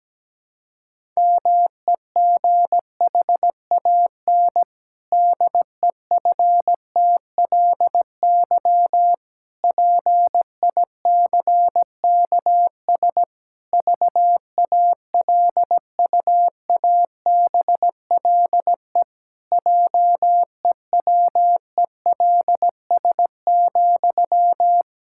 Morse code, heard as a keyed tone, about 17 wpm, 700 Hz. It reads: MEGHAN DEFTLY PICKS VALUABLE JEWELS,
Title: Pangram Challenge v2 - 17wpm